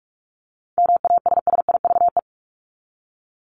Morse code, heard as a keyed tone, about 44 wpm, 700 Hz. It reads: MASSIVE
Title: Top 1751-2000 Words 45wpm